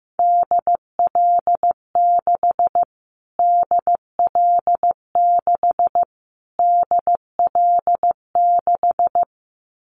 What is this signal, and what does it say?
Morse code, heard as a keyed tone, about 15 wpm, 700 Hz. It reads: DL6 DL6 DL6
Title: Callsign Prefixes for Top 25 DXCC - Repeated 3x - 15wpm